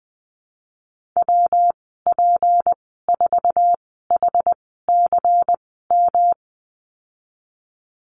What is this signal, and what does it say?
Morse code, heard as a keyed tone, about 20 wpm, spaced as for 15 wpm, 700 Hz. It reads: WP4HCM
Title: 173 Lesson 28g - Number 9 - Call Signs 20wpm